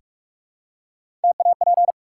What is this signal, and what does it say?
Morse code, heard as a keyed tone, about 44 wpm, 700 Hz. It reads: TAP
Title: Three Letter Words - 45wpm